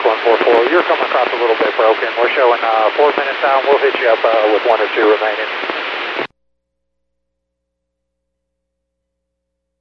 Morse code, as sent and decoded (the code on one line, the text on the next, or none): none